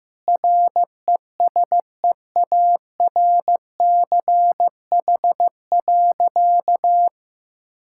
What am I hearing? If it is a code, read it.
Morse code, heard as a keyed tone, about 16 wpm, 700 Hz. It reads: RESEARCH.
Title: Sentences from Top 1000 Words - Review Part 5 - 15wpm